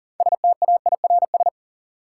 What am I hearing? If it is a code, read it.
Morse code, heard as a keyed tone, about 40 wpm, 700 Hz. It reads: STAIRS
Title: Sets of 5 Words Encore - New Word Drill - 40wpm